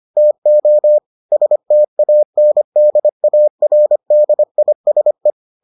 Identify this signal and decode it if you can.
Morse code, heard as a keyed tone, about 25 wpm, 600 Hz. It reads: TO STANDARDISE